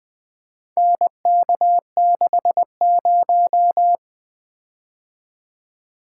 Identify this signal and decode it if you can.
Morse code, heard as a keyed tone, about 25 wpm, 700 Hz. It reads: NK60
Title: Speed-Racing - Four Letter-Number Combinations - 20wpm